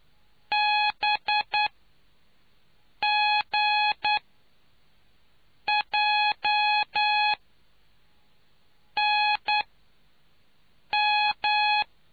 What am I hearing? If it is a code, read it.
Morse code, heard as a keyed tone, about 9 wpm, spaced as for 5 wpm, 840 Hz. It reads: BGJNM